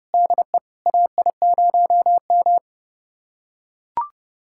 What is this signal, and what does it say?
Morse code, heard as a keyed tone, about 32 wpm, 700 Hz. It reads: DE AI0M
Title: USA Callsign DE Callsign - 4 Characters - 30wpm